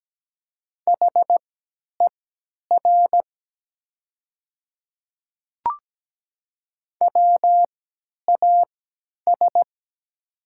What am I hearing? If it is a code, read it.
Morse code, heard as a keyed tone, about 17 wpm, spaced as for 10 wpm, 700 Hz. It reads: HER WAS
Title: 3x Character Spacing - Top 100 Words - 17wpm